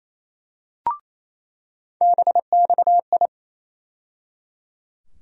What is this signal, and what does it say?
Morse code, heard as a keyed tone, about 28 wpm, 700 Hz. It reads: BXI
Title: USA Call Sign Postfixes - 3 Letters 28wpm